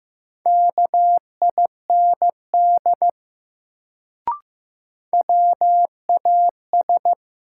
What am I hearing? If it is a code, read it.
Morse code, heard as a keyed tone, about 15 wpm, 700 Hz. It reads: KIND WAS
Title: Sets of 2 Words - New Word Drill - 15wpm